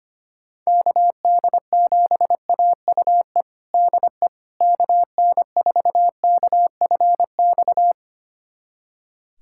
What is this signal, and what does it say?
Morse code, heard as a keyed tone, about 25 wpm, 700 Hz. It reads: KD7AUE DE KN4KFX
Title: USA Callsign DE Callsign - 6 Characters - 25wpm